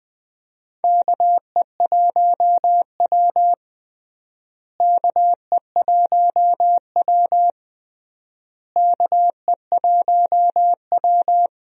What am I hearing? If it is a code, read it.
Morse code, heard as a keyed tone, about 21 wpm, 700 Hz. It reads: KE1W KE1W KE1W
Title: USA Call Signs - 4 Characters - Repeated 3x - 20wpm